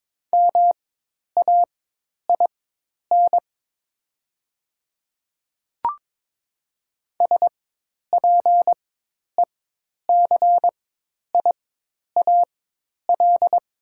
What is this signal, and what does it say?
Morse code, heard as a keyed tone, about 22 wpm, spaced as for 10 wpm, 700 Hz. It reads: MAIN SPECIAL
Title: 4x Character Spacing - Top 500 Words - 22wpm